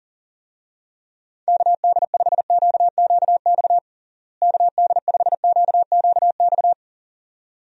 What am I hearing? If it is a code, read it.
Morse code, heard as a keyed tone, about 37 wpm, 700 Hz. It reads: KD5QQX KD5QQX
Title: USA Call Signs - 6 Characters - Repeated 2x - 40wpm